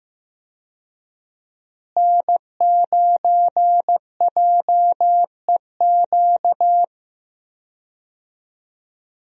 Morse code, heard as a keyed tone, about 15 wpm, 700 Hz. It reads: N9JEQ
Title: USA Call Signs - 5 Characters - Repeated 3x - 15wpm